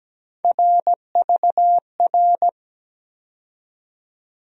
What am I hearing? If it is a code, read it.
Morse code, heard as a keyed tone, about 17 wpm, 700 Hz. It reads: RVR
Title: USA Call Sign Postfixes - 3 Letters 17wpm